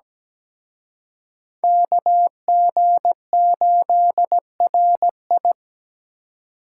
Morse code, heard as a keyed tone, about 20 wpm, 700 Hz. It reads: KG8RI
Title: Speed Racing - USA Call Signs 17wpm